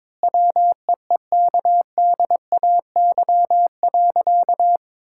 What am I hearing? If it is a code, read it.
Morse code, heard as a keyed tone, about 21 wpm, 700 Hz. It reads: WEEKDAY.
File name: Difficult Sentences with Hints - 22wpm